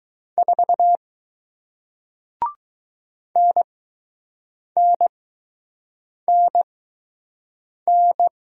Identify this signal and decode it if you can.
Morse code, heard as a keyed tone, about 20 wpm, spaced as for 6 wpm, 700 Hz. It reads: 4 NNNN